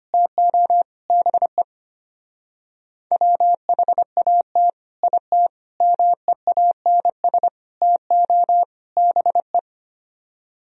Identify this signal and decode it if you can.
Morse code, heard as a keyed tone, about 29 wpm, 700 Hz. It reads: TO BE WHAT IT MEANS TO BE
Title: Speed-Racing - Sets of 5 Words 25wpm